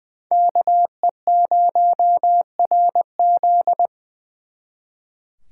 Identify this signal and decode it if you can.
Morse code, heard as a keyed tone, about 24 wpm, 700 Hz. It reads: KE0RZ
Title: Speed Racing - USA Call Signs 20wpm